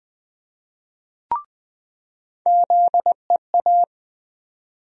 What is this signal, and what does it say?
Morse code, heard as a keyed tone, about 20 wpm, 700 Hz. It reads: ZEA
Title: USA Call Sign Postfixes - 3 Letters 20wpm